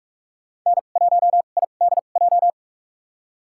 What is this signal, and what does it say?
Morse code, heard as a keyed tone, about 46 wpm, spaced as for 33 wpm, 700 Hz. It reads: N1IDJ
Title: USA Call Signs - 5 Characters - 2x Character Spacing - 45wpm